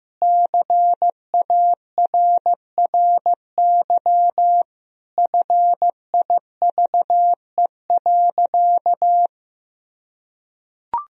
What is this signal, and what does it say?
Morse code, heard as a keyed tone, about 15 wpm, 700 Hz. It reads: CARRY FIVE.
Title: Sentences from Top 400 Words and Abbreviations 15wpm